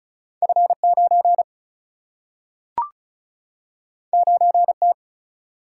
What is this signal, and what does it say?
Morse code, heard as a keyed tone, about 36 wpm, 700 Hz. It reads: F 9 9 T
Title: Single-Letter Single-Number 35wpm